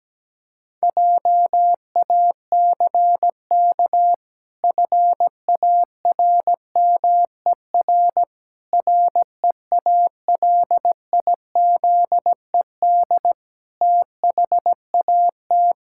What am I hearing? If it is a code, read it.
Morse code, heard as a keyed tone, about 17 wpm, 700 Hz. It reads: JACK FARMER REALIZED THAT
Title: Pangram Challenge v2 - 17wpm